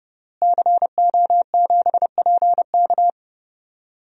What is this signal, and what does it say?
Morse code, heard as a keyed tone, about 30 wpm, 700 Hz. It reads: CO7PK
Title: DXCC-100 Call Signs - 5 Characters - Repeated 2x - 30wpm